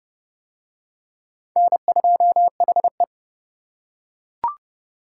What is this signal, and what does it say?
Morse code, heard as a keyed tone, about 28 wpm, 700 Hz. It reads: N2HE